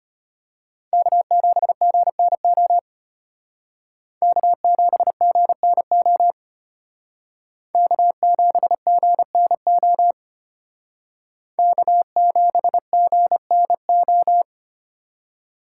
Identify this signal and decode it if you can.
Morse code, heard as a keyed tone, about 33 wpm, 700 Hz. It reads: K7GNO K7GNO K7GNO K7GNO